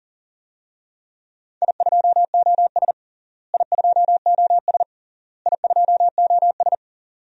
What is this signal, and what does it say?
Morse code, heard as a keyed tone, about 40 wpm, 700 Hz. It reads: I2OS I2OS I2OS
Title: DXCC-100 Call Signs - 4 Characters - Repeated 3x - 40wpm